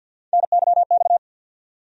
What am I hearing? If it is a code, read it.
Morse code, heard as a keyed tone, about 47 wpm, 700 Hz. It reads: NYX
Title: USA Call Sign Postfixes - 3 Letters 50wpm